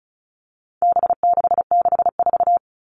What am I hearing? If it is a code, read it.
Morse code, heard as a keyed tone, about 34 wpm, 700 Hz. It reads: B664
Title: All Cut Numbers - 4-Digit 35wpm